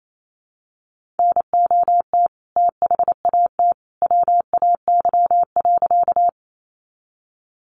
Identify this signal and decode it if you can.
Morse code, heard as a keyed tone, about 28 wpm, 700 Hz. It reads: NOT THAT WAY.
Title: A Sound Of Thunder 28wpm